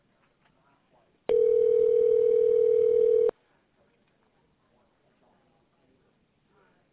Ringback tone (North America), 440 + 480 Hz.